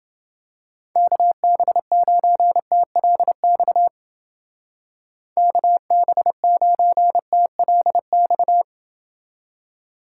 Morse code, heard as a keyed tone, about 28 wpm, 700 Hz. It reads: KB9TLX KB9TLX